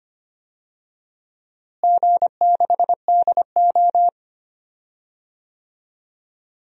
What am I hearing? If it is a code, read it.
Morse code, heard as a keyed tone, about 25 wpm, 700 Hz. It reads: G6DO
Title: DXCC-100 Call Signs - 4 Characters - Repeated 2x - 25wpm